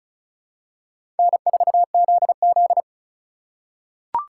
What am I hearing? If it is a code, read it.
Morse code, heard as a keyed tone, about 34 wpm, 700 Hz. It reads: N4ZZ
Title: USA Call Signs - 4 Characters - 2x Character Spacing - 35wpm